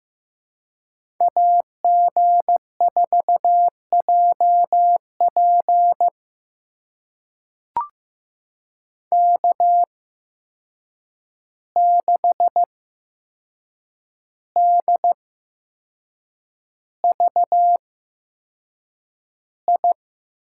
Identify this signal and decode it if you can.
Morse code, heard as a keyed tone, about 15 wpm, spaced as for 6 wpm, 700 Hz. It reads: AG4JP K6DVI
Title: USA Call Signs - 5 Characters - 8x Character Spacing - 15wpm